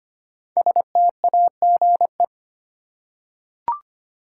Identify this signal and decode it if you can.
Morse code, heard as a keyed tone, about 25 wpm, 700 Hz. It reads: STAGE
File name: 5 Letter Words - First 3 Letters Given - 25wpm